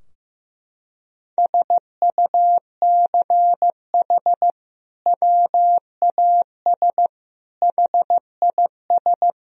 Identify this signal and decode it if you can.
Morse code, heard as a keyed tone, about 15 wpm, 700 Hz. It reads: SUCH WAS HIS
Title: Sentences from Top 1000 Words - Review Part 4 - 15wpm